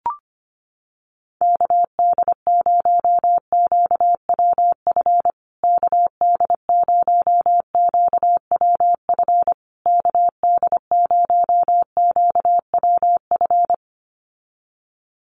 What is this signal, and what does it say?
Morse code, heard as a keyed tone, about 25 wpm, 700 Hz. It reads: KD0QWF KD0QWF KD0QWF
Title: USA Call Signs Repeated 3x 25wpm